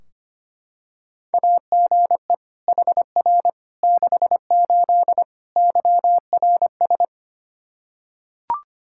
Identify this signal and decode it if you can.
Morse code, heard as a keyed tone, about 25 wpm, 700 Hz. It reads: AGE HR 68 YRS
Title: K1USN SST QSO Practice 25wpm